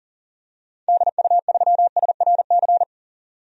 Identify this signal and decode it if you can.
Morse code, heard as a keyed tone, about 39 wpm, 700 Hz. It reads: DU3SRC
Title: DXCC-100 Call Signs - 6 Characters - 40wpm